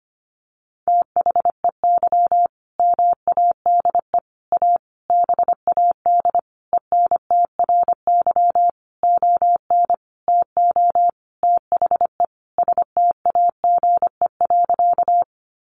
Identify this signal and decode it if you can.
Morse code, heard as a keyed tone, about 25 wpm, 700 Hz. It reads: THEY MADE A BAD ENTRY ON TO THE STAGE.
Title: Sentences from Top 1500 Words 25wpm